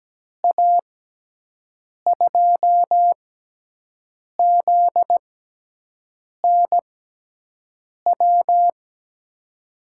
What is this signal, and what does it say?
Morse code, heard as a keyed tone, about 17 wpm, spaced as for 6 wpm, 700 Hz. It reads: A2ZNW
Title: USA Call Signs - 6 Characters - 6x Character Spacing - 17wpm